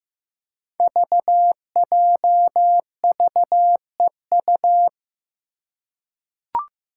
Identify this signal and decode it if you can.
Morse code, heard as a keyed tone, about 15 wpm, 700 Hz. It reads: VJVEU
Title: Five Challenging Letter-Number Combinations 15wpm